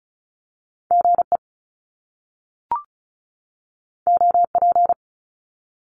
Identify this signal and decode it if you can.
Morse code, heard as a keyed tone, about 36 wpm, 700 Hz. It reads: G E O P
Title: Top Two Letter Combinations in Words 35wpm